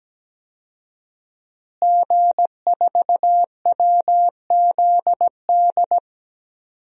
Morse code, heard as a keyed tone, about 17 wpm, 700 Hz. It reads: G4WZD
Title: DXCC-100 Call Signs - 5 Characters - Repeated 2x - 17wpm